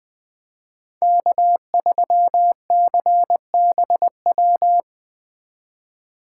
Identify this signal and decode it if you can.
Morse code, heard as a keyed tone, about 20 wpm, 700 Hz. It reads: K3CBW
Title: Speed Racing - USA 1x3 Call Signs 17wpm